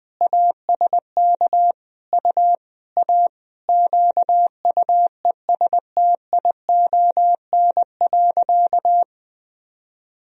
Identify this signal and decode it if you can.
Morse code, heard as a keyed tone, about 21 wpm, 700 Hz. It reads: ASK U A QUESTION.
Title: Sentences from Top 400 Words and Abbreviations 20wpm